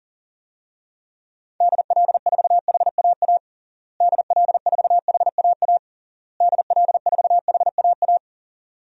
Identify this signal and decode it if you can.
Morse code, heard as a keyed tone, about 40 wpm, 700 Hz. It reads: DL4HAA DL4HAA DL4HAA
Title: DXCC-100 Call Signs - 6 Characters - Repeated 3x - 40wpm